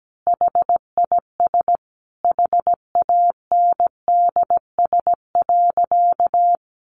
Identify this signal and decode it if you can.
Morse code, heard as a keyed tone, about 17 wpm, 700 Hz. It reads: HIS HANDS.
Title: A Sound Of Thunder 17wpm